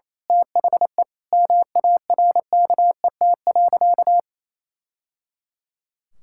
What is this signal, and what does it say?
Morse code, heard as a keyed tone, about 28 wpm, 700 Hz. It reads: THE MARKET.
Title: Sentences from Top 1000 Words - Review Part 5 - 28wpm